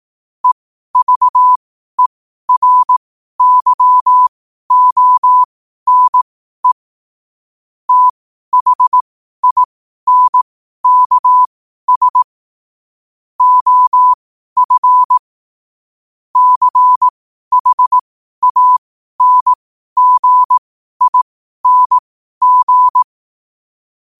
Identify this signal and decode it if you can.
Morse code, heard as a keyed tone, about 18 wpm, spaced as for 12 wpm, 1000 Hz. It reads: EVERYONE THINKS OF CHANGING
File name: qotdfiles / 12wpm